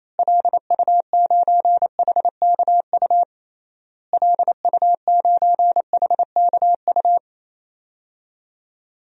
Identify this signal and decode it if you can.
Morse code, heard as a keyed tone, about 28 wpm, 700 Hz. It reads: LU9HKU LU9HKU
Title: DXCC-100 Call Signs - 6 Characters - Repeated 3x - 28wpm